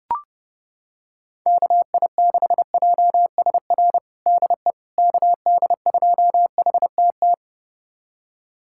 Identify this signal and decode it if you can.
Morse code, heard as a keyed tone, about 30 wpm, 700 Hz. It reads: KI6JSR DE KD2HTT
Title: USA Callsign DE Callsign - 6 Characters - 30wpm